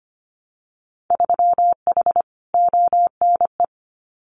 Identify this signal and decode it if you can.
Morse code, heard as a keyed tone, about 25 wpm, 700 Hz. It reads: 3H ONE
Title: Winter Field Day Exchanges Encore 25wpm